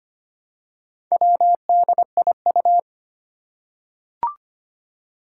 Morse code, heard as a keyed tone, about 25 wpm, 700 Hz. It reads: WDIU
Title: Four Challenging Letter-Number Combinations 25wpm